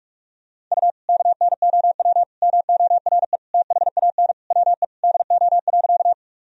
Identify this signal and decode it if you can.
Morse code, heard as a keyed tone, about 42 wpm, 700 Hz. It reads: U KNOW MORE THAN WE DO.
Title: Sentences from Top 100 Words and Abbreviations 45wpm